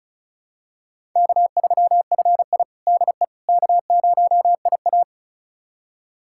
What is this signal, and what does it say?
Morse code, heard as a keyed tone, about 34 wpm, 700 Hz. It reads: K3FI DE K0IA
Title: USA Callsign DE Callsign - 4 Characters - 35wpm